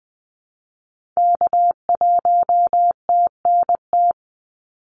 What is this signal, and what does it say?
Morse code, heard as a keyed tone, about 20 wpm, 700 Hz. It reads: K1TNT
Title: NA SOTA Activator Callsigns 20wpm